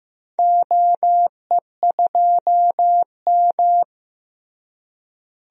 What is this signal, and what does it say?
Morse code, heard as a keyed tone, about 16 wpm, 700 Hz. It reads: OE2M
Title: DXCC-100 Call Signs - 4 Characters - Repeated 3x - 15wpm